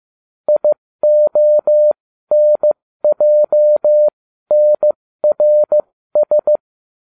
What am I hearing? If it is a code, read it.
Morse code, heard as a keyed tone, about 15 wpm, spaced as for 13 wpm, 600 Hz. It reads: IONJNRS